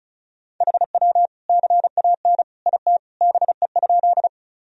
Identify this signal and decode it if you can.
Morse code, heard as a keyed tone, about 34 wpm, 700 Hz. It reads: HW CAN IT BE?
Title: Sentences from Top 100 Words and Abbreviations 35wpm